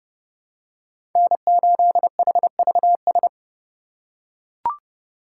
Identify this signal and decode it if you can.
Morse code, heard as a keyed tone, about 30 wpm, 700 Hz. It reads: N8HVS
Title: USA Call Signs - 5 Characters - 3x Character Spacing - 30wpm